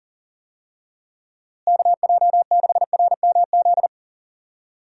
Morse code, heard as a keyed tone, about 38 wpm, 700 Hz. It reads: KJ6RMZ
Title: USA Call Signs - 6 Characters - Repeated 3x - 40wpm